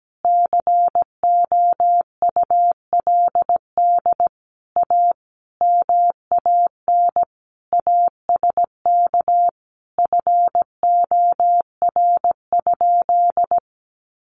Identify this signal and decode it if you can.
Morse code, heard as a keyed tone, about 17 wpm, 700 Hz. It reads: COULD A MAN ASK FOR?
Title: Sentences from Top 200 Words 17wpm